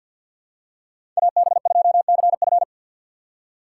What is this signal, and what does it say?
Morse code, heard as a keyed tone, about 49 wpm, 690 Hz. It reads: AB2CF
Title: USA Call Signs - 5 Characters - 2x Character Spacing - 50wpm